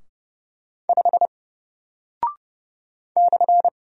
Morse code, heard as a keyed tone, about 30 wpm, 700 Hz. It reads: 5 /
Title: Single Character (Letter:Number:Punctuation:Prosign) Alt 30wpm